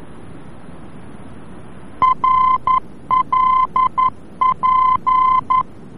Morse code, heard as a keyed tone, about 11 wpm, 1030 Hz. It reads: RLP
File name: france / mx / RLP-VOR-F